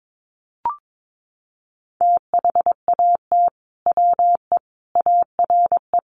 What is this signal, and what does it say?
Morse code, heard as a keyed tone, about 22 wpm, 700 Hz. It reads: THAT WE ARE
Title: Sets of 5 Words Encore 22wpm